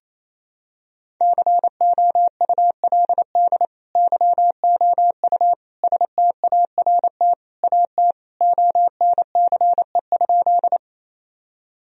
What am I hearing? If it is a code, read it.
Morse code, heard as a keyed tone, about 28 wpm, 700 Hz. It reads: COULD YOU START AT ONCE?
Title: Sentences from Top 1000 Words - Review Part 1 - 28wpm